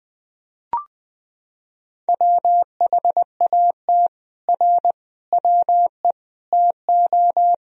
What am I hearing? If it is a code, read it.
Morse code, heard as a keyed tone, about 20 wpm, 700 Hz. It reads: WHAT R WE TO
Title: Sentences from Top 100 Words and Abbreviations 20wpm